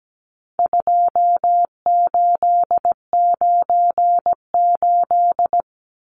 Morse code, heard as a keyed tone, about 17 wpm, 700 Hz. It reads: 2898
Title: Numbers - 4-Digit 17wpm